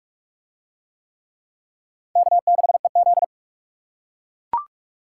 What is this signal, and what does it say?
Morse code, heard as a keyed tone, about 43 wpm, 700 Hz. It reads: K6EZ